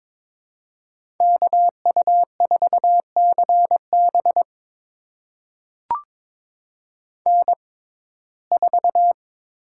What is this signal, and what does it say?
Morse code, heard as a keyed tone, about 22 wpm, spaced as for 16 wpm, 700 Hz. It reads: KU4CB N4
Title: USA Call Signs - 5 Characters - 6x Character Spacing - 22wpm